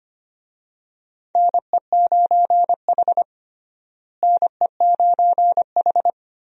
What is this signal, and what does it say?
Morse code, heard as a keyed tone, about 25 wpm, 700 Hz. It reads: NE9H NE9H
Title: USA Call Signs - 4 Characters - Repeated 3x - 25wpm